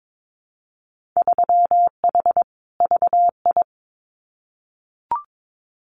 Morse code, heard as a keyed tone, about 22 wpm, 700 Hz. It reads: 3H VI
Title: Winter Field Day Exchanges Encore 22wpm